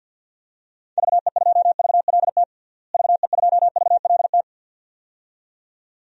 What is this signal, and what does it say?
Morse code, heard as a keyed tone, about 50 wpm, 700 Hz. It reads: VE2VLT VE2VLT
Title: DXCC-100 Call Signs - 6 Characters - Repeated 2x - 50wpm